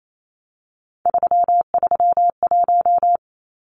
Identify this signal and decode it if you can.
Morse code, heard as a keyed tone, about 28 wpm, 700 Hz. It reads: 331
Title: Numbers - 3-Digit 28wpm